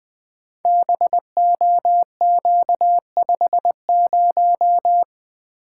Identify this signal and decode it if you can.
Morse code, heard as a keyed tone, about 21 wpm, 700 Hz. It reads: BOQ50
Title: Code Groups - 5 Letter:Numbers - 20wpm